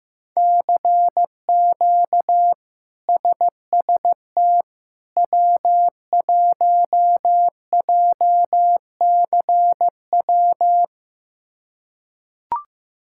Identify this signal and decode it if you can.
Morse code, heard as a keyed tone, about 15 wpm, 700 Hz. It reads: CQ SST W1JCW
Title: K1USN SST QSO Practice - 15wpm